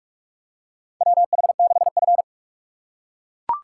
Morse code, heard as a keyed tone, about 42 wpm, 700 Hz. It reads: WH6F
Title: USA Call Signs - 4 Characters - 4x Character Spacing - 45wpm